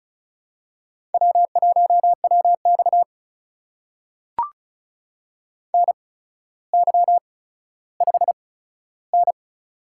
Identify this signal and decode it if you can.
Morse code, heard as a keyed tone, about 36 wpm, spaced as for 15 wpm, 700 Hz. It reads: W1WX NY5N
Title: USA Call Signs - 4 Characters - 8x Character Spacing - 35wpm